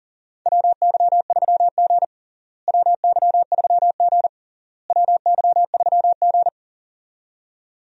Morse code, heard as a keyed tone, about 40 wpm, 700 Hz. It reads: WY3G WY3G WY3G
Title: USA Call Signs - 4 Characters - Repeated 3x - 40wpm